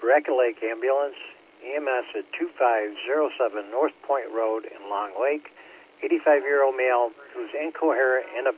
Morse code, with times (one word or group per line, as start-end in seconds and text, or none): none